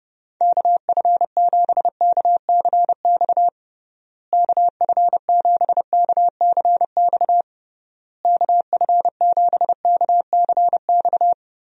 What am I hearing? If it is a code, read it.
Morse code, heard as a keyed tone, about 30 wpm, 700 Hz. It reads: KF7KCX KF7KCX KF7KCX
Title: USA Call Signs - 6 Characters - Repeated 3x - 30wpm